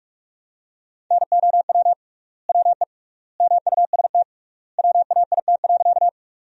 Morse code, heard as a keyed tone, about 46 wpm, 700 Hz. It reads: NOW WE MUST WAIT.
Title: 3x Word-Spacing - Sentences from Top 500 Words - 45wpm